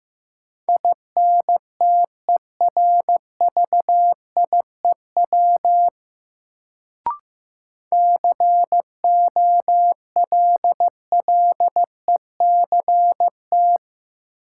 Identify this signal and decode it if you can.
Morse code, heard as a keyed tone, about 15 wpm, 700 Hz. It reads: INTERVIEW COLLECT
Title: Sentences from Top 1600 Words - New Word Drill Alt - 15wpm